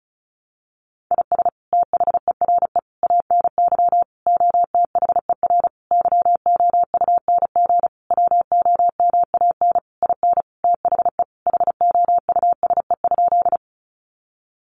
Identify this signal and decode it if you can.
Morse code, heard as a keyed tone, about 34 wpm, 700 Hz. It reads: IS THERE ANY OTHER YOUNG WOMAN IN THE HOUSE?
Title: Sentences from Top 1100 Words 35wpm